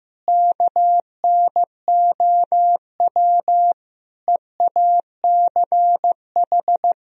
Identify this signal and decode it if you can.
Morse code, heard as a keyed tone, about 15 wpm, 700 Hz. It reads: KNOW EACH
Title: Sentences from Top 1000 Words - Review Part 1 - 15wpm